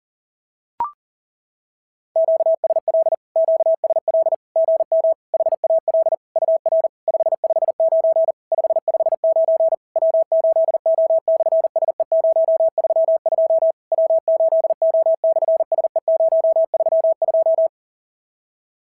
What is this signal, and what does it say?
Morse code, heard as a keyed tone, about 40 wpm, 650 Hz. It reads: QSL QSL GM HAL UR 559 559 W8O/SE032 W8O/SE032
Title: SOTA QSO Practice 40wpm